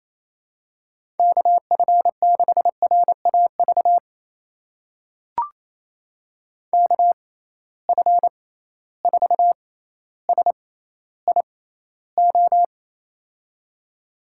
Morse code, heard as a keyed tone, about 28 wpm, spaced as for 14 wpm, 700 Hz. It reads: KF6RAV KF4SIO
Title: USA Call Signs - 6 Characters - 6x Character Spacing - 28wpm